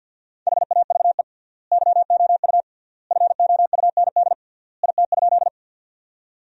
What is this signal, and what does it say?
Morse code, heard as a keyed tone, about 49 wpm, 700 Hz. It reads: HAVE YOU FOUND IT?
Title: 3x Word-Spacing - Sentences from Top 300 Words - 50wpm